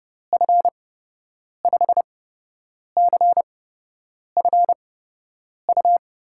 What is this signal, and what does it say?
Morse code, heard as a keyed tone, about 31 wpm, spaced as for 8 wpm, 700 Hz. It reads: F5CFU
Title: USA Call Signs - 6 Characters - 8x Character Spacing - 30wpm